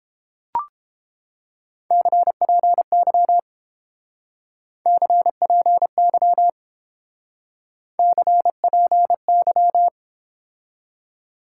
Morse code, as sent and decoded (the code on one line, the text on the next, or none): -.-. .--. -.-- / -.-. .--. -.-- / -.-. .--. -.--
CPY CPY CPY